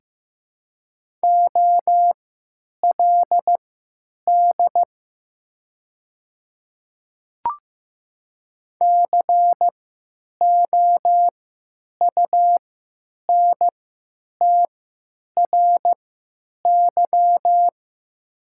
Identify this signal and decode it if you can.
Morse code, heard as a keyed tone, about 15 wpm, spaced as for 9 wpm, 700 Hz. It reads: OLD COUNTRY
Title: 3x Character Spacing - Top 300 Words - 15wpm